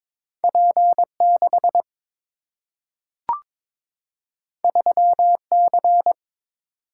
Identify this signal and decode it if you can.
Morse code, heard as a keyed tone, about 23 wpm, 700 Hz. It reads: P6 3C
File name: practice / 22wpm / Two Letter-Number Combinations - 22wpm